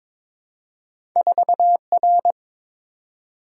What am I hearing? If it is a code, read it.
Morse code, heard as a keyed tone, about 22 wpm, 700 Hz. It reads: 4R
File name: practice / 22wpm / Two Letter-Number Combinations - 22wpm